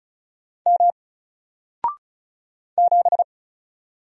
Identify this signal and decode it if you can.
Morse code, heard as a keyed tone, about 33 wpm, 700 Hz. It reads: M 7
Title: Single Letter-Number Alt 35wpm